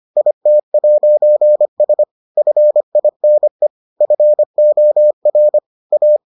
Morse code, heard as a keyed tone, about 25 wpm, 600 Hz. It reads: IT'S FINE FOR A